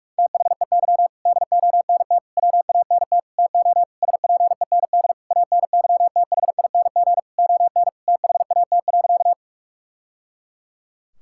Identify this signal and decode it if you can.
Morse code, heard as a keyed tone, about 46 wpm, 700 Hz. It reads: THEY DONT WANT TO SPEND ANYTHING ON THAT.
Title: Sentences from Top 1000 Words - Review Part 5 - 45wpm